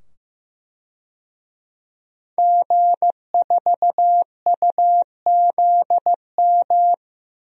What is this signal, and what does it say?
Morse code, heard as a keyed tone, about 15 wpm, 700 Hz. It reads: G4UZM
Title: DXCC-100 Call Signs - 5 Characters - 15wpm